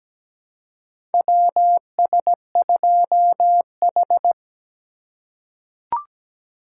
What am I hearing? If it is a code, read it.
Morse code, heard as a keyed tone, about 17 wpm, 700 Hz. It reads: WS2H
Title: USA Call Signs - 4 Characters - 4x Character Spacing - 17wpm